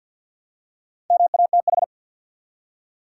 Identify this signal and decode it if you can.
Morse code, heard as a keyed tone, about 51 wpm, 700 Hz. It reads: MATH
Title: Four Letter Words - 50wpm